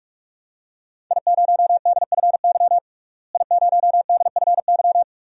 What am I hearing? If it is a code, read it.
Morse code, heard as a keyed tone, about 48 wpm, 700 Hz. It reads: I0DFY I0DFY
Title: DXCC-100 Call Signs - 5 Characters - Repeated 3x - 45wpm